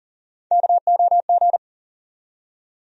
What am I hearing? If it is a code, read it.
Morse code, heard as a keyed tone, about 40 wpm, 700 Hz. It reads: KOG
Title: USA Call Sign Postfixes - 3 Letters 40wpm